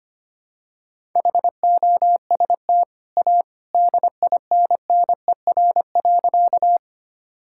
Morse code, heard as a keyed tone, about 25 wpm, 690 Hz. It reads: HOST A DINNER.
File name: Easy Sentences with Hints - 25wpm